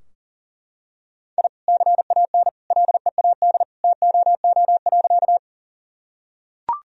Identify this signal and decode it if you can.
Morse code, heard as a keyed tone, about 37 wpm, 700 Hz. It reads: I CAN LEAD TOO.
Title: Sentences from Top 1000 Words - Review Part 2 - 40wpm